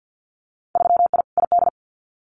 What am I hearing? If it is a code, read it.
Morse code, heard as a keyed tone, about 50 wpm, 700 Hz. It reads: 3I ID
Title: Winter Field Day Exchanges Encore 50wpm